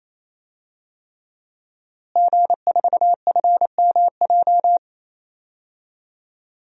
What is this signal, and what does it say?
Morse code, heard as a keyed tone, about 28 wpm, 700 Hz. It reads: G4FMJ
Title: DXCC-100 Call Signs - 5 Characters - Repeated 3x - 28wpm